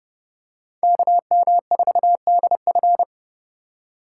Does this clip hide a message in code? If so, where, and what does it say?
Yes: Morse code from 0.8 to 3.0 s, heard as a keyed tone, about 29 wpm, 700 Hz. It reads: KM4DF